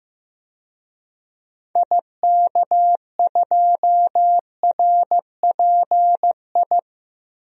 Morse code, heard as a keyed tone, about 15 wpm, 700 Hz. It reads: IK2RPI
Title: DXCC-100 Call Signs - 6 Characters - Repeated 2x - 15wpm